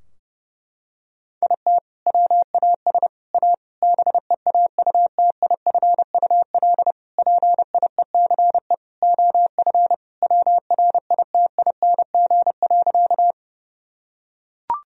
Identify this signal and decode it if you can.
Morse code, heard as a keyed tone, about 32 wpm, 700 Hz. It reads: IT WAS A BEAUTIFUL PIECE OF WRITING.
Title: Sentences from Top 1000 Words - Review Part 2 - 30wpm